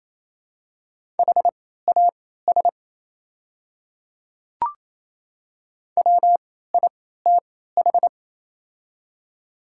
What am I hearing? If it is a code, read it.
Morse code, heard as a keyed tone, about 27 wpm, spaced as for 16 wpm, 700 Hz. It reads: HAS WITH
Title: 3x Character Spacing - Top 100 Words - 28wpm